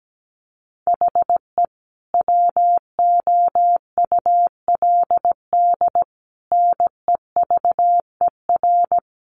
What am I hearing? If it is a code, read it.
Morse code, heard as a keyed tone, about 17 wpm, 700 Hz. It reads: HE WOULD NEVER
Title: Sentences from Top 1300 Words 17wpm